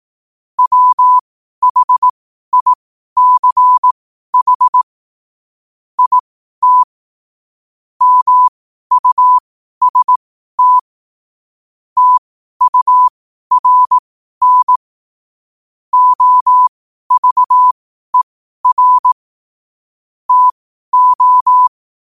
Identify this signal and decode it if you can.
Morse code, heard as a keyed tone, about 18 wpm, spaced as for 12 wpm, 1000 Hz. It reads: WHICH IT MUST TURN OVER TO
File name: qotdfiles / 12wpm